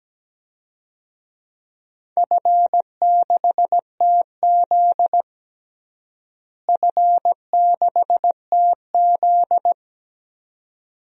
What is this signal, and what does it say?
Morse code, heard as a keyed tone, about 17 wpm, 700 Hz. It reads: F6TZ F6TZ
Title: DXCC-100 Call Signs - 4 Characters - Repeated 2x - 17wpm